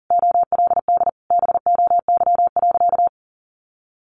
Morse code, heard as a keyed tone, about 40 wpm, 700 Hz. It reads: OLD BOY.
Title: Sentences from Top 1500 Words 40wpm